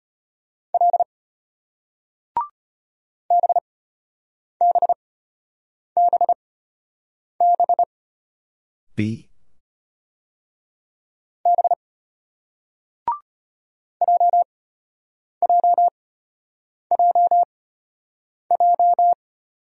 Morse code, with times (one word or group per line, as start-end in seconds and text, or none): 0.7-1.0 L
3.3-7.8 BBBB
11.5-11.7 B
14.0-19.1 JJJJ